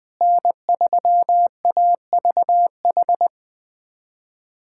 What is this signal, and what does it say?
Morse code, heard as a keyed tone, about 20 wpm, 700 Hz. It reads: N3AVH
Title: USA Callsign DE Callsign - 5 Characters - 20wpm